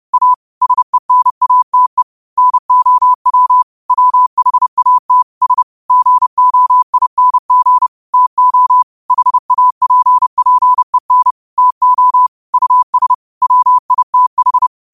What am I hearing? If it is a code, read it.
Morse code, heard as a keyed tone, about 30 wpm, 1000 Hz. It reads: A SENATE NOW WHAT S GOING TO HAPPEN TO US WITH
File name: qotdfiles / 30wpm